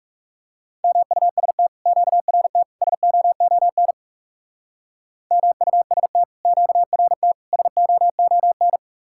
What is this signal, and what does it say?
Morse code, heard as a keyed tone, about 36 wpm, 700 Hz. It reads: MUST QRT SOON MUST QRT SOON
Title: Speed-Racing Common QSO Phrases - 30wpm